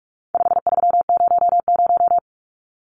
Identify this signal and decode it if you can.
Morse code, heard as a keyed tone, about 44 wpm, 700 Hz. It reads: 5300
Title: Numbers - 4-Digit 45wpm